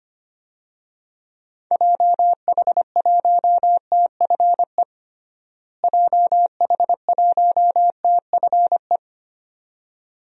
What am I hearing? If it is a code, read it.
Morse code, heard as a keyed tone, about 25 wpm, 700 Hz. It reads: JH1TFE JH1TFE
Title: DXCC-100 Call Signs - 6 Characters - Repeated 2x - 25wpm